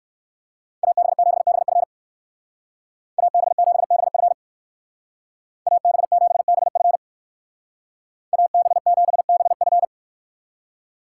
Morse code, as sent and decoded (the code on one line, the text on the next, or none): .- -... --... -... ..-. / .- -... --... -... ..-. / .- -... --... -... ..-. / .- -... --... -... ..-.
AB7BF AB7BF AB7BF AB7BF